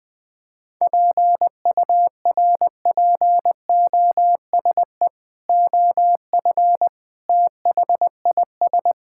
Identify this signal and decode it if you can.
Morse code, heard as a keyed tone, about 21 wpm, 700 Hz. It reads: PURPOSE OF THIS
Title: Build A Phrase - 5-Word Phrases - 20wpm